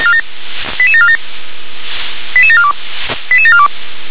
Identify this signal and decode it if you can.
Five-tone selective call (ZVEI), heard as quick runs of five tones, tones of about 70 ms. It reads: 88747 88742 88742